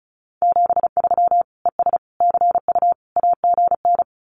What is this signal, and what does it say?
Morse code, heard as a keyed tone, about 34 wpm, 700 Hz. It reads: 73 ES CU AGN
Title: Common QSO Phrases Encore 35wpm